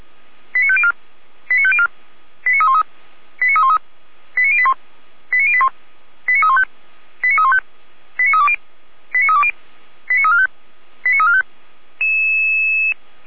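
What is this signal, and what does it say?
Five-tone selective call (ZVEI), heard as quick runs of five tones, tones of about 70 ms. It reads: 89504 89504 89313 89313 89081 89081 89315 89315 89320 89320 89345 89345
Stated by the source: Five-tone ZVEI selective call 89340, received on an SDR.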